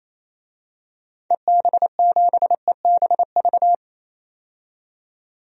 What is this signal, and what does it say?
Morse code, heard as a keyed tone, about 28 wpm, 700 Hz. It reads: EB7EBV
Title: DXCC-100 Call Signs - 6 Characters - 28wpm